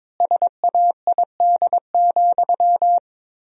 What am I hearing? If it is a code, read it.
Morse code, heard as a keyed tone, about 22 wpm, 700 Hz. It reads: SAID,
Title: 4x Word-Spacing - Sentences from Top 300 Words - 22wpm